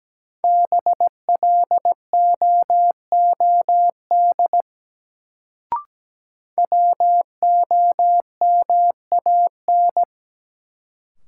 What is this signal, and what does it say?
Morse code, heard as a keyed tone, about 17 wpm, 700 Hz. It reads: BLOOD WOMAN
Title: Sentences from Top 1100 Words - New Word Drill Alt - 17wpm